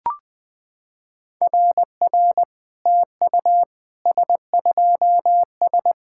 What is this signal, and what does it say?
Morse code, heard as a keyed tone, about 20 wpm, 700 Hz. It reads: RR TU S2S
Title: SOTA QSO Practice 20wpm